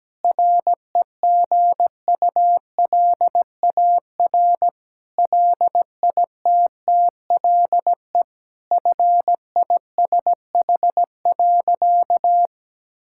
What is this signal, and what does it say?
Morse code, heard as a keyed tone, about 17 wpm, 700 Hz. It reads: REGULAR LITTLE FISH.
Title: Sentences from Top 1000 Words - Review Part 6 - 17wpm